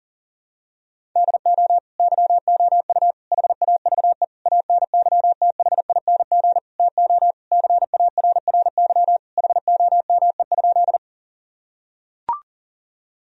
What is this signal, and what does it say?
Morse code, heard as a keyed tone, about 40 wpm, 700 Hz. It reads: DO YOU HAVE ANYTHING TO CARRY HOME?
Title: Sentences from Top 1000 Words - Review Part 1 - 40wpm